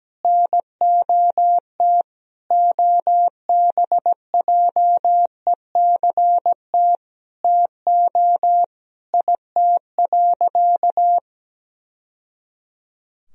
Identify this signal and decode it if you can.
Morse code, heard as a keyed tone, about 17 wpm, 700 Hz. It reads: NOT OBJECT TO IT.
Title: Sentences from Top 1000 Words - Review Part 3 - 17wpm